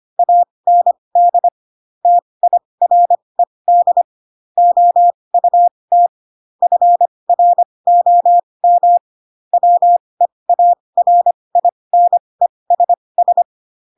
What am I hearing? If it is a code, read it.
Morse code, heard as a keyed tone, about 25 wpm, spaced as for 20 wpm, 700 Hz. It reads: AND TIRED OUT FROM WEARINESS
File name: Pinocchio / Pinocchio ch6 15 25 ik1hge